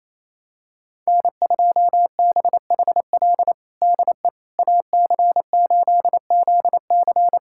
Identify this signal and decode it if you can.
Morse code, heard as a keyed tone, about 27 wpm, 700 Hz. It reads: N2BHL DE AC8ZC
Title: USA Callsign DE Callsign - 5 Characters - 28wpm